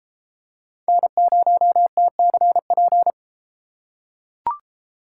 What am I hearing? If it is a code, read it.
Morse code, heard as a keyed tone, about 33 wpm, 700 Hz. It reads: N0TCP